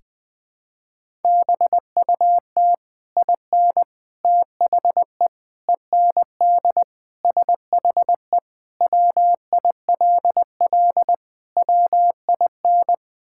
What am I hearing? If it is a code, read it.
Morse code, heard as a keyed tone, about 20 wpm, 700 Hz. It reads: BUT IN THE END SHE WILL WIN
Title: Sentences with 3-Word Phrases - 20wpm